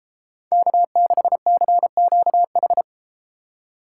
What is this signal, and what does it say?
Morse code, heard as a keyed tone, about 30 wpm, 700 Hz. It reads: K6CQH